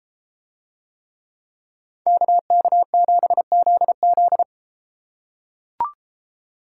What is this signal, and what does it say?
Morse code, heard as a keyed tone, about 33 wpm, 700 Hz. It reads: KK7ZZ